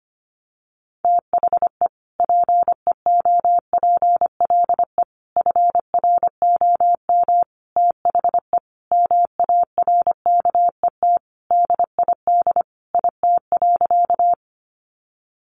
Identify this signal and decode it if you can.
Morse code, heard as a keyed tone, about 25 wpm, 700 Hz. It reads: THE PEOPLE FROM THE MARKET DID IT.
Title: Sentences from Top 600 Words 25wpm